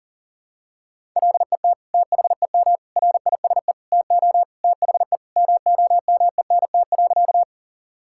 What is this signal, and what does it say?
Morse code, heard as a keyed tone, about 40 wpm, 700 Hz. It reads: LET THEM RISE TO THE MOMENT.
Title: Sentences from Top 1000 Words - Review Part 6 - 40wpm